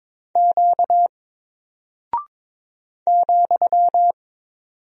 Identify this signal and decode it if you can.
Morse code, heard as a keyed tone, about 23 wpm, 700 Hz. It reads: Q ,
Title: Single Character (Letter:Number:Punctuation:Prosign) 22wpm